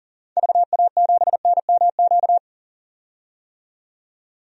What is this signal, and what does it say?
Morse code, heard as a keyed tone, about 40 wpm, 700 Hz. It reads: VA7NMQ
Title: DXCC-100 Call Signs - 6 Characters - 40wpm